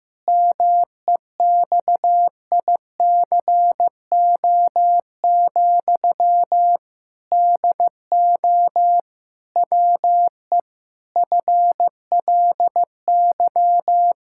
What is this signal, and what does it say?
Morse code, heard as a keyed tone, about 15 wpm, 700 Hz. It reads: MEXICO, DO WE FLY
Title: Pangram Challenge v2 - 15wpm